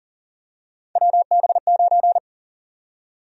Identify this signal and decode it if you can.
Morse code, heard as a keyed tone, about 40 wpm, 700 Hz. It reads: WB9
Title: USA Call Sign Prefixes - Repeated 2x - 40wpm